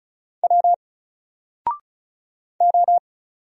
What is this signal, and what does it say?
Morse code, heard as a keyed tone, about 33 wpm, 700 Hz. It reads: W O
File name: Single-Letter-Number-35wpm